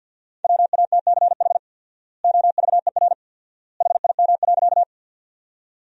Morse code, heard as a keyed tone, about 50 wpm, 700 Hz. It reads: WATCH OVER HIM.
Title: 4x Word-Spacing - Sentences from Top 300 Words - 50wpm